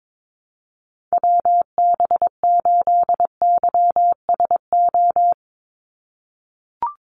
Morse code, heard as a keyed tone, about 22 wpm, 700 Hz. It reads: WB8YSO
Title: USA Call Signs 22wpm